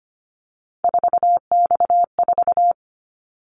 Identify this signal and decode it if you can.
Morse code, heard as a keyed tone, about 25 wpm, 700 Hz. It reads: 4X4
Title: Callsign Prefixes for Top 50 DXCC - 25wpm